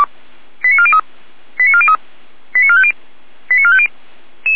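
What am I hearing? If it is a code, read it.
Five-tone selective call (ZVEI), heard as quick runs of five tones, tones of about 80 ms. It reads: 89403 89403 89450 89450
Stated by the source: ZVEI five-tone selective call received on an SDR.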